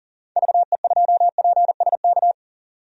Morse code, heard as a keyed tone, about 40 wpm, 700 Hz. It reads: VE2PSK
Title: DXCC-100 Call Signs - 6 Characters - 40wpm